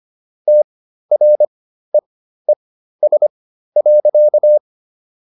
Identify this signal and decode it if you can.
Morse code, heard as a keyed tone, about 25 wpm, spaced as for 13 wpm, 600 Hz. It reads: TREES.